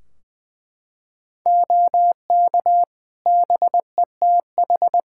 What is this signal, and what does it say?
Morse code, heard as a keyed tone, about 21 wpm, 700 Hz. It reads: OK BETH
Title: Practice QSO - Exchange 3 - More Info - 20wpm